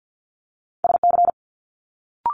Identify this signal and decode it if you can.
Morse code, heard as a keyed tone, about 50 wpm, 700 Hz. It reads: SC
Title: Multi-letter Phonemes Quick Fire 50wpm